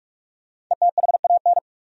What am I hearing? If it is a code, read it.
Morse code, heard as a keyed tone, about 46 wpm, 700 Hz. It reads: ETHAN
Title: USA Common Names - 45wpm